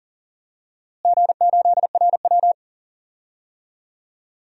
Morse code, heard as a keyed tone, about 38 wpm, 700 Hz. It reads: G8RW
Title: DXCC-100 Call Signs - 4 Characters - Repeated 2x - 40wpm